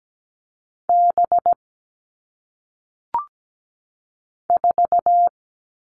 Morse code, heard as a keyed tone, about 17 wpm, 700 Hz. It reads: B 4
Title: All Cut Numbers - 1-Digit 17wpm